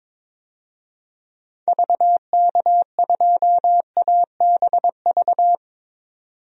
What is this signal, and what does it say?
Morse code, heard as a keyed tone, about 23 wpm, 700 Hz. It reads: VK2ABV
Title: DXCC-100 Call Signs - 6 Characters - 22wpm